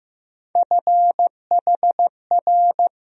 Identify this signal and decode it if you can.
Morse code, heard as a keyed tone, about 15 wpm, 700 Hz. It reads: FHR
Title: USA Call Sign Postfixes - 3 Letters 15wpm